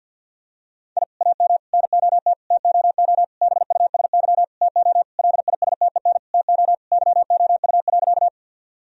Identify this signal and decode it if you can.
Morse code, heard as a keyed tone, about 50 wpm, 700 Hz. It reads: I AM NOT TOO BUSY TO LISTEN TO YOU.
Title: Sentences from Top 1000 Words - Review Part 3 - 50wpm